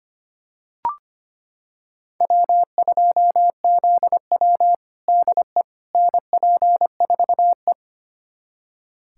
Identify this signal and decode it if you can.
Morse code, heard as a keyed tone, about 25 wpm, 700 Hz. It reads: W2ZW DE NP4E
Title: USA Callsign DE Callsign - 4 Characters - 25wpm